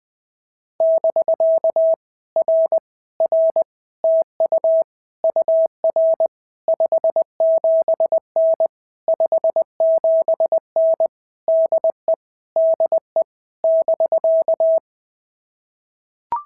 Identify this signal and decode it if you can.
Morse code, heard as a keyed tone, about 20 wpm, 650 Hz. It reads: <BK> R R TU UR 57N 57N DE DE <BK>
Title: POTA QSO Practice - 20wpm